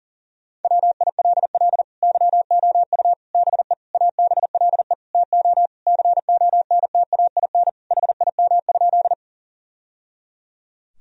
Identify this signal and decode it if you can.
Morse code, heard as a keyed tone, about 40 wpm, 700 Hz. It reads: WILL YOU BE ABLE TO CONTAIN HIM?
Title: Sentences from Top 1000 Words - Review Part 2 - 40wpm